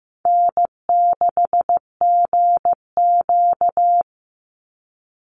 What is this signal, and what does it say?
Morse code, heard as a keyed tone, about 15 wpm, 700 Hz. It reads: N6GQ
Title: Easy Call Signs - 15wpm